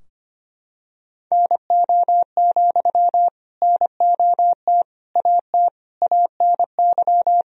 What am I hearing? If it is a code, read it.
Morse code, heard as a keyed tone, about 25 wpm, 700 Hz. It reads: NO, NOT AT ANY
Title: Sentences from Top 1000 Words - Review Part 4 - 25wpm